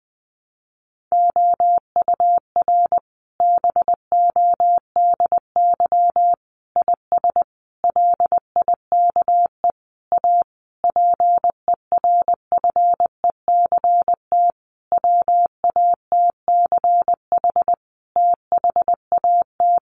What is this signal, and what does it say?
Morse code, heard as a keyed tone, about 20 wpm, 700 Hz. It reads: OUR BODY IS LIKE A PERFECT WATCH THAT
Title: Sentences from Top 1000 Words 20wpm